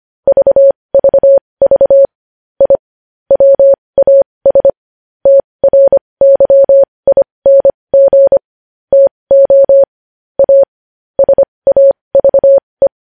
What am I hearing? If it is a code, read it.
Morse code, heard as a keyed tone, about 25 wpm, spaced as for 19 wpm, 550 Hz. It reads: VVV I WAS TRYING TO AS AVE
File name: podcast / News headlines-1x-25x20